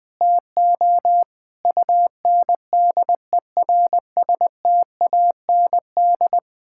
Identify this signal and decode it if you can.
Morse code, heard as a keyed tone, about 20 wpm, 700 Hz. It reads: TO UNDERSTAND
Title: Sentences with 8-Word Phrases - 20wpm